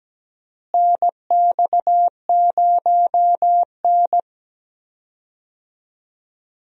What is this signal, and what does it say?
Morse code, heard as a keyed tone, about 17 wpm, 700 Hz. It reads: NX0N
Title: USA Call Signs - 4 Characters - Repeated 2x - 17wpm